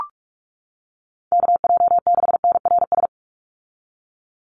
Morse code, heard as a keyed tone, about 44 wpm, 700 Hz. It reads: KJ6NRS
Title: USA Call Signs 45wpm